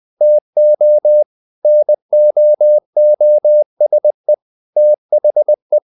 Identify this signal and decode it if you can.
Morse code, heard as a keyed tone, about 20 wpm, 600 Hz. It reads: TO NOOSE THE
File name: the old man and the sea part 2 20wpm